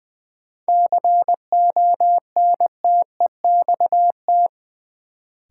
Morse code, heard as a keyed tone, about 20 wpm, 700 Hz. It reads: CONTEXT
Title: Sets of 4 Words - New Word Drill - 20wpm